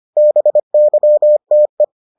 Morse code, heard as a keyed tone, about 25 wpm, 600 Hz. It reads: BYTE